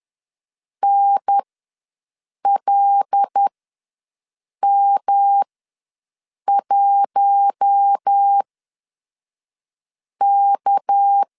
Morse code, heard as a keyed tone, about 10 wpm, spaced as for 5 wpm, 800 Hz. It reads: NLM1K
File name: LEZIONI / 7 GF9Z / 7 GF9 ws 10wpm 4034